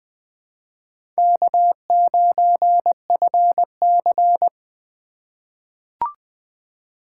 Morse code, heard as a keyed tone, about 20 wpm, 700 Hz. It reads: K9FC
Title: USA Call Signs - 2x Character Spacing - 20wpm